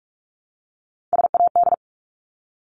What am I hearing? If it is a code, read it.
Morse code, heard as a keyed tone, about 44 wpm, 700 Hz. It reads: SAD
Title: Top 1501-1750 Words 45wpm